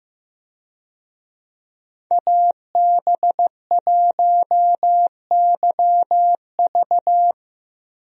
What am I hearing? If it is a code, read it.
Morse code, heard as a keyed tone, about 15 wpm, 700 Hz. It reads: AB1YV
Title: USA Call Signs - 5 Characters - Repeated 2x - 15wpm